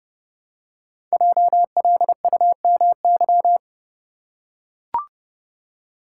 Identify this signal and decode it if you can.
Morse code, heard as a keyed tone, about 30 wpm, 700 Hz. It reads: JLUMY
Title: Code Groups - 5 Letters - 30wpm